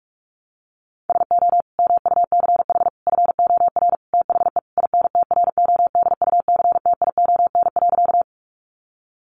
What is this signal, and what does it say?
Morse code, heard as a keyed tone, about 44 wpm, 700 Hz. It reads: SO MUCH FOR THE INTRODUCTION.
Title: Sentences from Top 1000 Words 45wpm